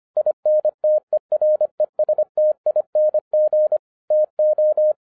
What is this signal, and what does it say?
Morse code, heard as a keyed tone, about 25 wpm, 610 Hz. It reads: INTERESTING TO
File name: 2011-10-mp3 / n09-CW TO PS2 CONVERTER 25wpm